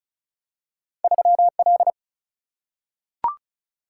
Morse code, heard as a keyed tone, about 36 wpm, 700 Hz. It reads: 3L
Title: Single-Letter Single-Number 35wpm